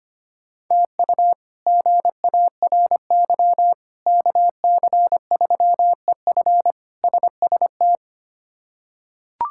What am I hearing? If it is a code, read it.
Morse code, heard as a keyed tone, about 25 wpm, 700 Hz. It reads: TU GARY KC3EF SST
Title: K1USN SST QSO Practice - 25wpm